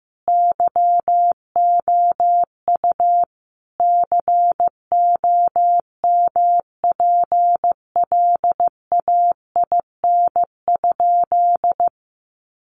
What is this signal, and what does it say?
Morse code, heard as a keyed tone, about 15 wpm, 700 Hz. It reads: YOU COMPLAIN?
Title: Sentences from Top 1300 Words 15wpm